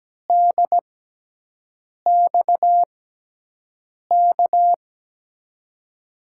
Morse code, heard as a keyed tone, about 17 wpm, spaced as for 6 wpm, 700 Hz. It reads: DXK
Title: USA Call Signs - 5 Characters - 6x Character Spacing - 17wpm